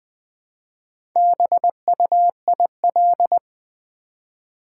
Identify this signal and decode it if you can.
Morse code, heard as a keyed tone, about 20 wpm, 700 Hz. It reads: BUIL
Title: Build A Word - 2 Letters At A Time - 8-Letter Words - 20wpm